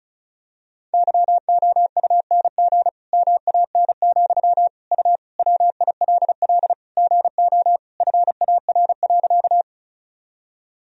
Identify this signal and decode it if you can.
Morse code, heard as a keyed tone, about 36 wpm, 700 Hz. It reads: YOUNG MAN, U WILL GO FAR.
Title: Sentences from Top 400 Words and Abbreviations 35wpm